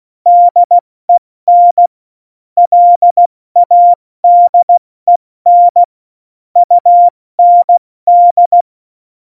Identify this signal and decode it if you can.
Morse code, heard as a keyed tone, about 16 wpm, 700 Hz. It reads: DEN LADEN UND
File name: Morsekurs / Witze / Witz 23. Oktober 2018, 70 F80